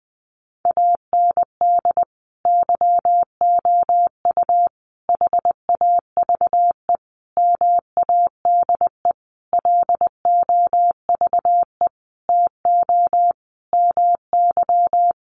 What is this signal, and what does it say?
Morse code, heard as a keyed tone, about 19 wpm, 700 Hz. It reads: AND YOU HAVE MADE LOVE TO MY
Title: Sentences from Top 1500 Words 20wpm